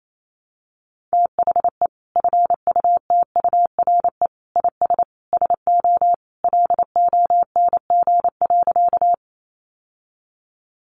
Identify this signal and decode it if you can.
Morse code, heard as a keyed tone, about 28 wpm, 700 Hz. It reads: THE FUTURE IS SO LONG.
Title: Sentences from Top 700 Words 28wpm